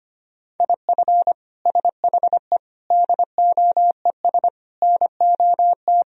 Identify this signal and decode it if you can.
Morse code, heard as a keyed tone, about 25 wpm, 700 Hz. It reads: IF SHE DOES NOT
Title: Sentences from Top 1000 Words - Review Part 3 - 25wpm